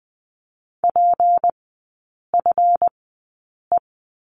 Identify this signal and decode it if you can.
Morse code, heard as a keyed tone, about 20 wpm, spaced as for 8 wpm, 700 Hz. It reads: PFE
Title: 152 Lesson 25e - Letter p - Three Characters 20wpm